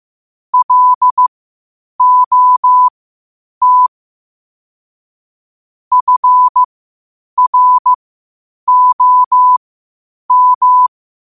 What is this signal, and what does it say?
Morse code, heard as a keyed tone, about 15 wpm, spaced as for 8 wpm, 1000 Hz. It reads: LOT FROM
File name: qotdfiles / 8wpm